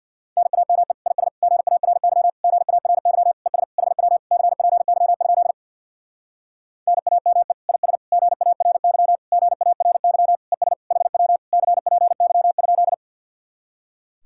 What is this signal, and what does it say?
Morse code, heard as a keyed tone, about 64 wpm, 700 Hz. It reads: NAME IS GARY GARY ES HW CPY? NAME IS GARY GARY ES HW CPY?